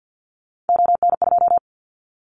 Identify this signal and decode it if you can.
Morse code, heard as a keyed tone, about 50 wpm, 700 Hz. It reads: ON2
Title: Callsign Prefixes for Top 25 DXCC - 50wpm